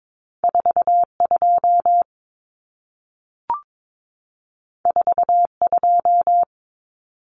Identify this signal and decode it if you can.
Morse code, heard as a keyed tone, about 22 wpm, 700 Hz. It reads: 42 42
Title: Numbers - 2-Digit 22wpm